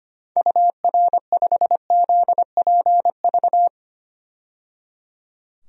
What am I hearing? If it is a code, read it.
Morse code, heard as a keyed tone, about 25 wpm, 700 Hz. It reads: UR5ZPV
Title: DXCC-100 Call Signs - 6 Characters - 25wpm